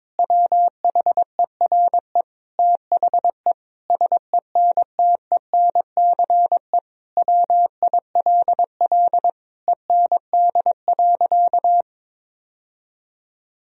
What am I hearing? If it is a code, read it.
Morse code, heard as a keyed tone, about 22 wpm, 700 Hz. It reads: WHERE THE SENTENCE WILL END.
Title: Sentences from Top 1000 Words - Review Part 1 - 22wpm